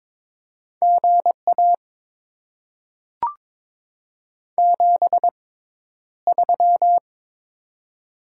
Morse code, heard as a keyed tone, about 22 wpm, spaced as for 11 wpm, 700 Hz. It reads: GA 73
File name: Top 20 QSO Elements - 6x Character Spacing - 22wpm